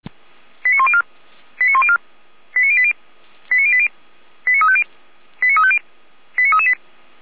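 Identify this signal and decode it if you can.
Five-tone selective call (ZVEI), heard as quick runs of five tones, tones of about 70 ms. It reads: 89104 89104 89080 89080 89350 89350 89338
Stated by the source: ZVEI five-tone selective call received on an SDR.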